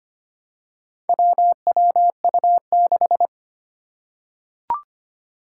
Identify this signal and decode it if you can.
Morse code, heard as a keyed tone, about 25 wpm, 700 Hz. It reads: WWU6
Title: Four Challenging Letter-Number Combinations 25wpm